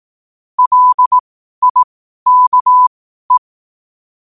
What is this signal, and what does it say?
Morse code, heard as a keyed tone, about 18 wpm, spaced as for 12 wpm, 1000 Hz. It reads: LIKE
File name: qotdfiles / 12wpm